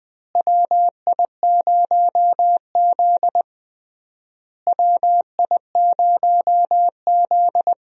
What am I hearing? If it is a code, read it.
Morse code, heard as a keyed tone, about 20 wpm, 700 Hz. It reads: WI0Z WI0Z
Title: USA Call Signs - 4 Characters - Repeated 3x - 20wpm